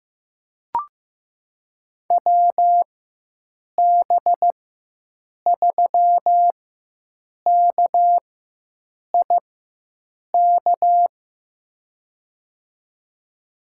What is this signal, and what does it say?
Morse code, heard as a keyed tone, about 15 wpm, spaced as for 7 wpm, 700 Hz. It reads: WB3KIK
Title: USA Call Signs - 6 Characters - 4x Character Spacing - 15wpm